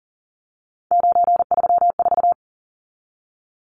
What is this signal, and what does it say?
Morse code, heard as a keyed tone, about 40 wpm, 700 Hz. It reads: 934
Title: Numbers - 3-Digit 40wpm